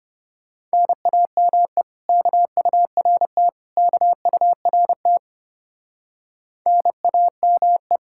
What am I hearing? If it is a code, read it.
Morse code, heard as a keyed tone, about 33 wpm, 700 Hz. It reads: NAME KURT KURT NAME